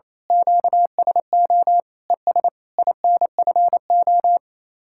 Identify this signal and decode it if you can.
Morse code, heard as a keyed tone, about 27 wpm, 700 Hz. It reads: QSO ES INFO
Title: Speed-Racing Common QSO Phrases - 28wpm